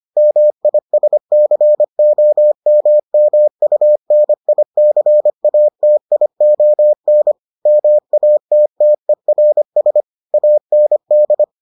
Morse code, heard as a keyed tone, about 25 wpm, 600 Hz. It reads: MISCOMMUNICATION MATTERS AND